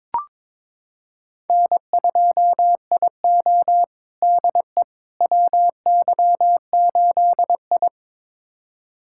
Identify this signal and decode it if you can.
Morse code, heard as a keyed tone, about 22 wpm, 700 Hz. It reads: N2IO DE WY8I
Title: USA Callsign DE Callsign - 4 Characters - 22wpm